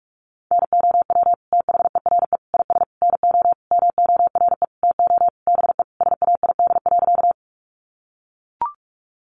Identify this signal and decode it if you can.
Morse code, heard as a keyed tone, about 44 wpm, 700 Hz. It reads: NOW THERE IS NO MORE TO BE SAID.
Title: Sentences from Top 100 Words 45wpm